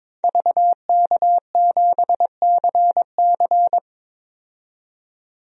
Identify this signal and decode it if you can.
Morse code, heard as a keyed tone, about 22 wpm, 700 Hz. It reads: VK7CC
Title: DXCC-100 Call Signs - 5 Characters - 22wpm